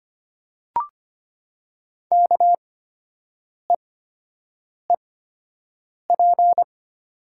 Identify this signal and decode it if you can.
Morse code, heard as a keyed tone, about 25 wpm, spaced as for 7 wpm, 690 Hz. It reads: KEEP
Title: 8x Character Spacing - Top 500 Words - 25wpm